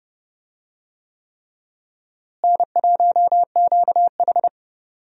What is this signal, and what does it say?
Morse code, heard as a keyed tone, about 30 wpm, 700 Hz. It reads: N1QH
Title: USA Call Signs - 4 Characters - Repeated 3x - 30wpm